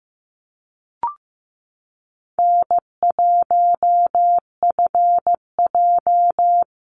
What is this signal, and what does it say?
Morse code, heard as a keyed tone, about 15 wpm, 700 Hz. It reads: N1FJ
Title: NA SOTA Activator Callsigns 15wpm